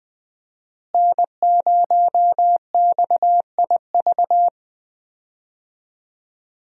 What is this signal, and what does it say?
Morse code, heard as a keyed tone, about 20 wpm, 700 Hz. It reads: N0XIV
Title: USA Call Signs - 5 Characters - Repeated 3x - 20wpm